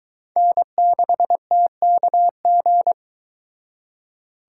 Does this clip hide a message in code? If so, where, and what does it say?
Yes: Morse code from 0.4 to 2.9 s, heard as a keyed tone, about 23 wpm, 700 Hz. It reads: N6TKG